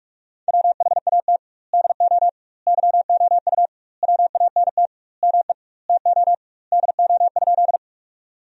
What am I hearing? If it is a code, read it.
Morse code, heard as a keyed tone, about 41 wpm, 700 Hz. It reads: WHAT DO YOU WANT ME TO DO?
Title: 2x Word-Spacing - Sentences from Top 200 Words - 45wpm